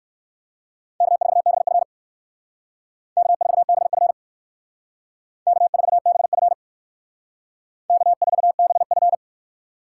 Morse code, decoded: K4BF K4BF K4BF K4BF

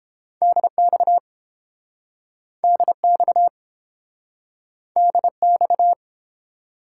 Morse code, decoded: DX DX DX

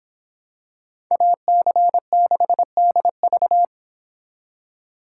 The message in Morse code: AC6DV

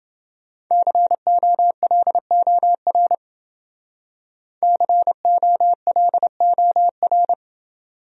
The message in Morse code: COLOR COLOR

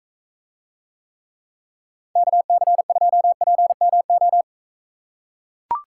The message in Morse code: KC2PMO

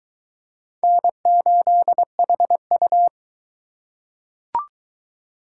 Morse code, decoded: N8HU

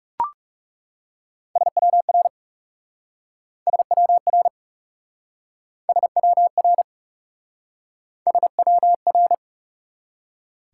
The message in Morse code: SWR SWR SWR SWR

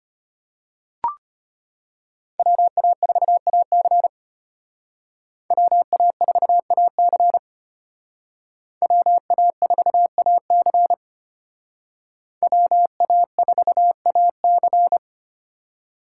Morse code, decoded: WA4AC WA4AC WA4AC WA4AC

